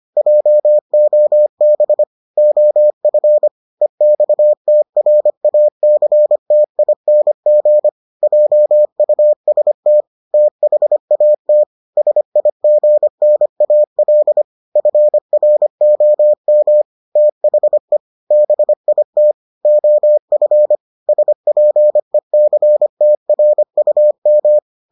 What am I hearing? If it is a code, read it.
Morse code, heard as a keyed tone, about 25 wpm, 600 Hz. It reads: JOB OF EXTRACTING JUST THAT SIGNAL FROM THE BIT OF SPECTRUM